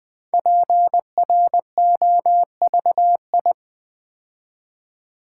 Morse code, heard as a keyed tone, about 20 wpm, 700 Hz. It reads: PROVI
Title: Build A Word - 5 Letters At A Time - 10-Letter Words - 20wpm